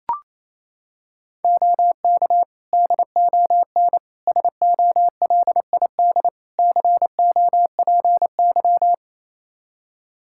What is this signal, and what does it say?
Morse code, heard as a keyed tone, about 28 wpm, 700 Hz. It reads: OK DON SOLID COPY
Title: Practice QSO - Exchange 3 - More Info - 28wpm